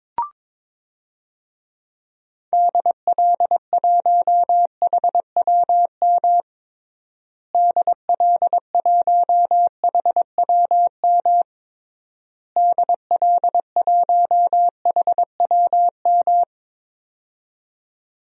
Morse code, heard as a keyed tone, about 22 wpm, 700 Hz. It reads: DL1HWM DL1HWM DL1HWM
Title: DXCC-100 Call Signs - 6 Characters - Repeated 3x - 22wpm